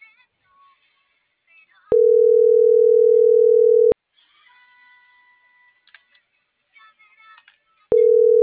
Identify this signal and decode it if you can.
Ringback tone (North America), 440 + 480 Hz.